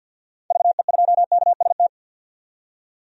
Morse code, heard as a keyed tone, about 52 wpm, 700 Hz. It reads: VE2KST
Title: DXCC-100 Call Signs - 6 Characters - 50wpm